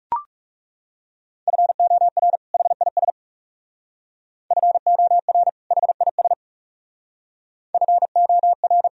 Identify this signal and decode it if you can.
Morse code, heard as a keyed tone, about 36 wpm, 700 Hz. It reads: FOR HIS FOR HIS FOR
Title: Speed-Racing - Sets of 2 Words 30wpm